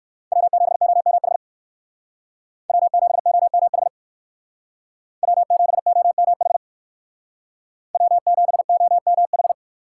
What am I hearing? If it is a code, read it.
Morse code, heard as a keyed tone, about 58 wpm, 700 Hz. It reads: W7OMH W7OMH W7OMH W7OMH